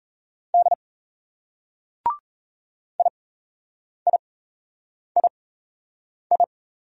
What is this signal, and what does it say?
Morse code, heard as a keyed tone, about 38 wpm, spaced as for 8 wpm, 700 Hz. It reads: D IIII